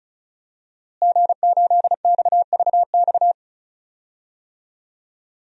Morse code, heard as a keyed tone, about 34 wpm, 700 Hz. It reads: G8XVX
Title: DXCC-100 Call Signs - 5 Characters - Repeated 2x - 35wpm